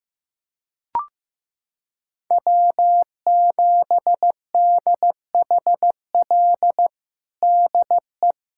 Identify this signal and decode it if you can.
Morse code, heard as a keyed tone, about 16 wpm, 700 Hz. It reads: W7DHL DE
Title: USA Callsign DE Callsign - 5 Characters - 15wpm